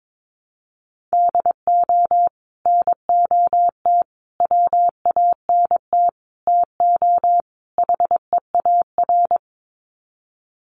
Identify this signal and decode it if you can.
Morse code, heard as a keyed tone, about 22 wpm, 700 Hz. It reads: DO NOT WANT TO HEAR
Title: Sets of 5 Words 22wpm